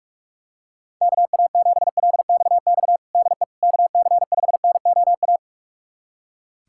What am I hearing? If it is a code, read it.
Morse code, heard as a keyed tone, about 46 wpm, 700 Hz. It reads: KA7LXX DE KC5NOA
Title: USA Callsign DE Callsign 45wpm